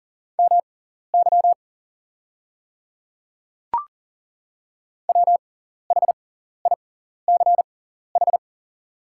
Morse code, heard as a keyed tone, about 40 wpm, spaced as for 14 wpm, 690 Hz. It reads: MY WHICH
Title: 6x Character Spacing - Top 100 Words - 40wpm